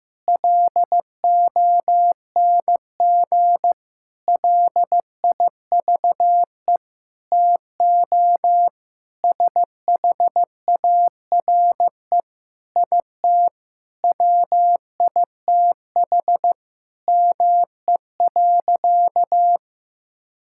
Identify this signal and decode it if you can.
Morse code, heard as a keyed tone, about 15 wpm, 700 Hz. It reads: LONG LIVE TO SHARE IT WITH ME.
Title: Sentences from Top 1000 Words - Review Part 6 - 15wpm